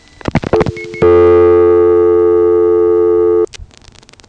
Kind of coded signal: telephone call progress tone